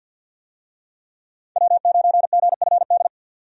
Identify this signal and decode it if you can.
Morse code, heard as a keyed tone, about 53 wpm, 700 Hz. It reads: W9GFD
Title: USA Call Signs - 5 Characters - 50wpm